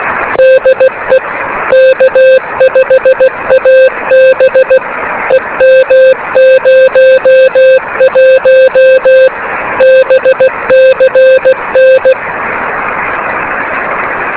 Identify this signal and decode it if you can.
Morse code, heard as a keyed tone, about 16 wpm, 520 Hz, with noise. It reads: DE K5AB EM01 BCN
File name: beacons / k5ab